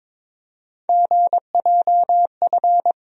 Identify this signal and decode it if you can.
Morse code, heard as a keyed tone, about 22 wpm, 700 Hz. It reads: GJF
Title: USA Call Sign Postfixes - 3 Letters 22wpm